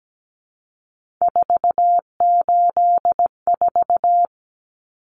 Morse code, heard as a keyed tone, about 17 wpm, 700 Hz. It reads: 484
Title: Numbers - 3-Digit 17wpm